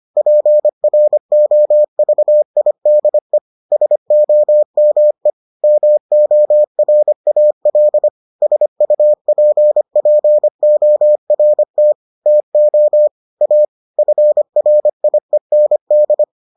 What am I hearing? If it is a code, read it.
Morse code, heard as a keyed tone, about 25 wpm, 600 Hz. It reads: PROVIDE SOME MORAL SUPPORT TO A FRIEND